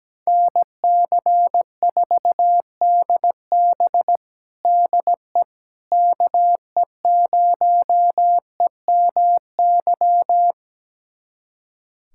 Morse code, heard as a keyed tone, about 17 wpm, 700 Hz. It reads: NC4DB DE KE0EMY
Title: USA Callsign DE Callsign 17wpm